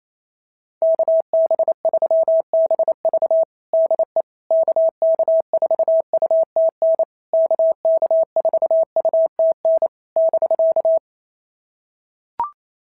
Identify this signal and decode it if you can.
Morse code, heard as a keyed tone, about 27 wpm, 650 Hz. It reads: KB3BV DE KK4UTN KK4UTN <BK>
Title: SOTA QSO Practice 28wpm